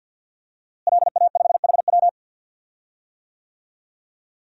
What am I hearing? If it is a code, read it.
Morse code, heard as a keyed tone, about 47 wpm, 700 Hz. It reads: LA5HW
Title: DXCC-100 Call Signs - 5 Characters - Repeated 3x - 50wpm